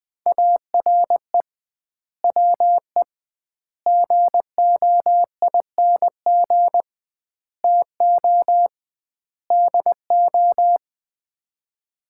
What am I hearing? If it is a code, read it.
Morse code, heard as a keyed tone, about 21 wpm, 700 Hz. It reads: ARE WE GOING TO DO
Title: 2x Word-Spacing - Sets of 5 Words - 20wpm